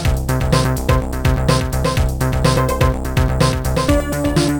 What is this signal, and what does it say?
There is no code.